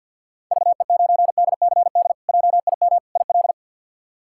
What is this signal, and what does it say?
Morse code, heard as a keyed tone, about 52 wpm, 690 Hz. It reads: VE9DCD JIM IL
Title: K1USN SST QSO Practice - 50wpm